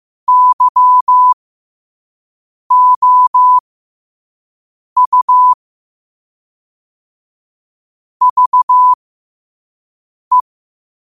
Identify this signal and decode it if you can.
Morse code, heard as a keyed tone, about 15 wpm, spaced as for 5 wpm, 1000 Hz. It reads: YOU VE